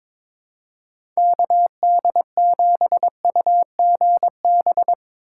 Morse code, heard as a keyed tone, about 22 wpm, 700 Hz. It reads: KD7UGB
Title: USA Call Signs - 4x Character Spacing - 22wpm